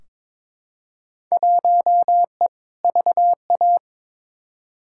Morse code, heard as a keyed tone, about 22 wpm, 700 Hz. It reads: 1E VA
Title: Field Day Exchanges - 22wpm